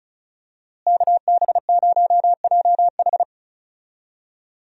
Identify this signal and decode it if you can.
Morse code, heard as a keyed tone, about 36 wpm, 700 Hz. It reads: KB0JH
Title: USA Call Signs - 5 Characters - Repeated 3x - 35wpm